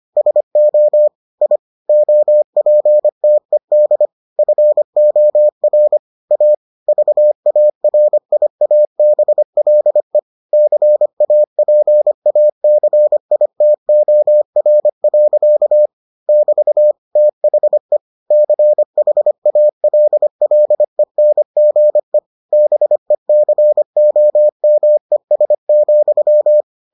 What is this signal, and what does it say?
Morse code, heard as a keyed tone, about 25 wpm, 600 Hz. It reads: SO I OPTED FOR A VARIABLE CAPACITOR. <BT> THE CHALLENGE BECOMES,